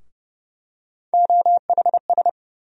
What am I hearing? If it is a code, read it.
Morse code, heard as a keyed tone, about 30 wpm, 700 Hz. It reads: OHS
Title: USA Call Sign Postfixes - 3 Letters 30wpm